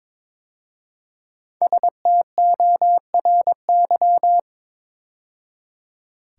Morse code, heard as a keyed tone, about 22 wpm, 700 Hz. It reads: STORY
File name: 5 Letter Words - First 3 Letters Given - 22wpm